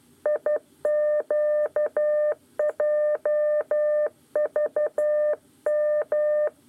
Morse code, heard as a keyed tone, about 12 wpm, 600 Hz. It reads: IQJVM
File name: 8 lezione cw 1